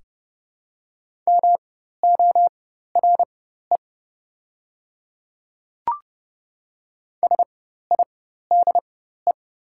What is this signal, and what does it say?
Morse code, heard as a keyed tone, about 30 wpm, spaced as for 14 wpm, 700 Hz. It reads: MORE SIDE
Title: 4x Character Spacing - Top 100 Words - 30wpm